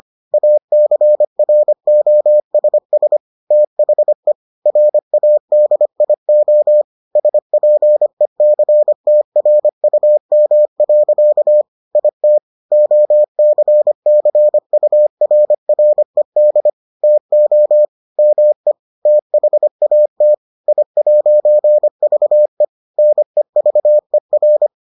Morse code, heard as a keyed tone, about 25 wpm, 600 Hz. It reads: ACROSS THE RADIO SPECTRUM. IT OCCURRED TO ME THAT I'VE NEVER